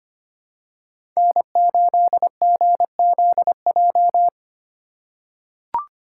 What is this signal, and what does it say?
Morse code, heard as a keyed tone, about 25 wpm, 700 Hz. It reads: N8GZJ
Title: USA Call Signs - 5 Characters - 8x Character Spacing - 25wpm